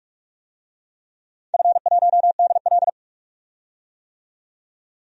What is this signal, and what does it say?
Morse code, heard as a keyed tone, about 46 wpm, 700 Hz. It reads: F1DL
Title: DXCC-100 Call Signs - 4 Characters - 45wpm